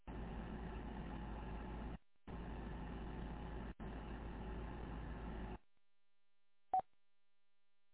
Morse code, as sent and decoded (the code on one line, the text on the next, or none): none